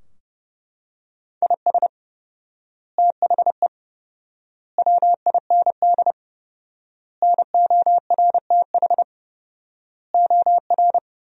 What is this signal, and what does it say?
Morse code, heard as a keyed tone, about 30 wpm, 700 Hz. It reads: IS THE WIND NORTH OR
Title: 4x Word-Spacing - Sentences from Top 400 Words - 30wpm